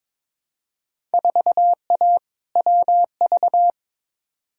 Morse code, heard as a keyed tone, about 22 wpm, 700 Hz. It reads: 4A WV
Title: Field Day Exchanges Encore - 22wpm